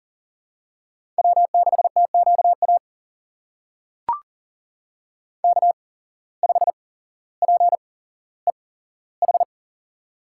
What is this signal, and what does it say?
Morse code, heard as a keyed tone, about 39 wpm, spaced as for 17 wpm, 700 Hz. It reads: W6TQA K5PEH
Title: USA Call Signs - 5 Characters - 8x Character Spacing - 40wpm